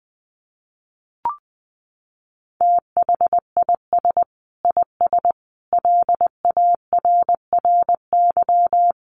Morse code, heard as a keyed tone, about 20 wpm, 700 Hz. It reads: THIS IS LARRY
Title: Sets of 5 Words Encore2 20wpm